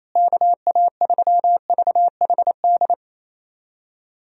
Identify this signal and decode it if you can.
Morse code, heard as a keyed tone, about 28 wpm, 700 Hz. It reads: KA3VHD
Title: USA Call Signs - 6 Characters - Repeated 2x - 28wpm